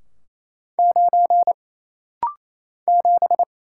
Morse code, heard as a keyed tone, about 28 wpm, 700 Hz. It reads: 9 7
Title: Single Letter-Number Alt - Rapid-Fire 28wpm